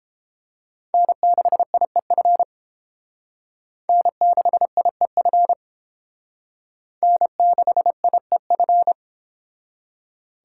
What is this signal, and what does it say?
Morse code, heard as a keyed tone, about 31 wpm, 700 Hz. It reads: N6IEF N6IEF N6IEF